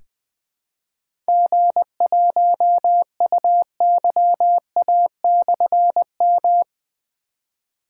Morse code, heard as a keyed tone, about 20 wpm, 700 Hz. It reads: G1UYA/M
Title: Difficult Call Signs 20wpm